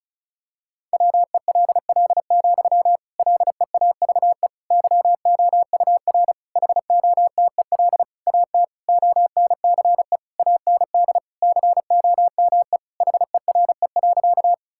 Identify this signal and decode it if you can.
Morse code, heard as a keyed tone, about 34 wpm, 700 Hz. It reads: WELL, LEAVE YOUR HOTEL AT ONCE AND COME HERE.
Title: Sentences from Top 1600 Words 35wpm